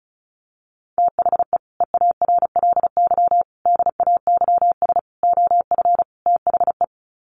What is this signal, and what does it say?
Morse code, heard as a keyed tone, about 34 wpm, 700 Hz. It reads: THE EARLY DAYS OF THE
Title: Sets of 5 Words Encore 35wpm